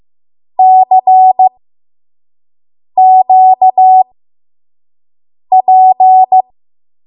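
Morse code, heard as a keyed tone, about 15 wpm, spaced as for 5 wpm, 750 Hz. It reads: CQP
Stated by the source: 5 WPM Morse code practice.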